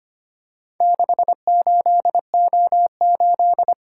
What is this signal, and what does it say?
Morse code, heard as a keyed tone, about 25 wpm, 700 Hz. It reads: 68O8
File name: practice / 25wpm / Four Letter-Number Combinations - 25wpm